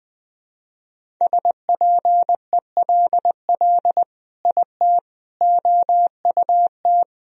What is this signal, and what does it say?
Morse code, heard as a keyed tone, about 20 wpm, 700 Hz. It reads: SPELL IT OUT
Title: Sentences from Top 1000 Words - Review Part 1 - 20wpm